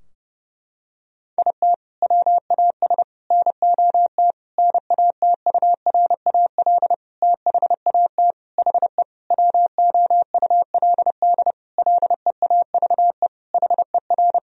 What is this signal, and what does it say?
Morse code, heard as a keyed tone, about 30 wpm, 700 Hz. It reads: IT WAS NOT NATURAL THAT HE WOULD LEAVE HER
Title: Sentences from Top 1000 Words - Review Part 4 - 30wpm